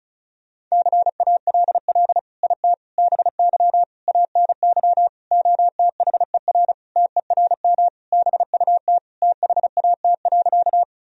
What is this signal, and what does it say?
Morse code, heard as a keyed tone, about 36 wpm, 700 Hz. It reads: CALL IT BY ANY OTHER TERM BUT THAT.
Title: Sentences from Top 1000 Words - Review Part 4 - 35wpm